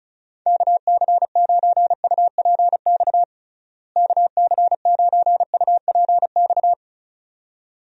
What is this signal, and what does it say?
Morse code, heard as a keyed tone, about 36 wpm, 700 Hz. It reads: KC9UPX KC9UPX
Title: USA Call Signs - 6 Characters - Repeated 2x - 35wpm